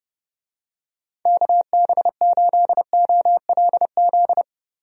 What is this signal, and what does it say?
Morse code, heard as a keyed tone, about 30 wpm, 700 Hz. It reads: KB8OLZ
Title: USA Call Signs - 6 Characters - Repeated 3x - 30wpm